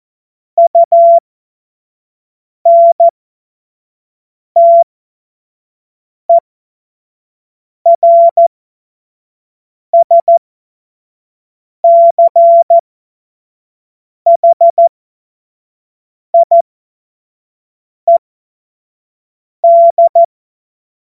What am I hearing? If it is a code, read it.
Morse code, heard as a keyed tone, about 14 wpm, spaced as for 5 wpm, 680 Hz. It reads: UNTERSCHIED
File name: Witz-14-Oktober-2025-25-F70